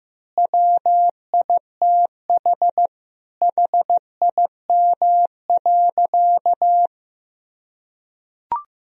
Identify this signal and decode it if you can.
Morse code, heard as a keyed tone, about 15 wpm, 700 Hz. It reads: WITH HIM.
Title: Sentences from Top 1000 Words - Review Part 4 - 15wpm